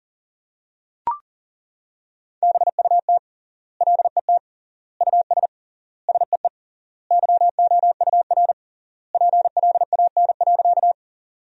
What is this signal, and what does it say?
Morse code, heard as a keyed tone, about 40 wpm, 700 Hz. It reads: BUT LET US SEE YOUR PLAN.
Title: 3x Word-Spacing - Sentences from Top 500 Words - 40wpm